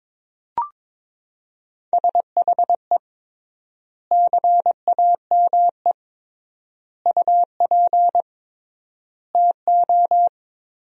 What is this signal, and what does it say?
Morse code, heard as a keyed tone, about 22 wpm, 700 Hz. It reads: SHE CAME UP TO
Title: 3x Word-Spacing - Sentences from Top 200 Words - 22wpm